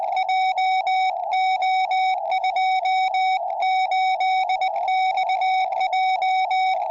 Morse code, heard as a keyed tone, about 20 wpm, 2250 Hz, with radio noise. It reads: JO28XJ
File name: tropo / la3eq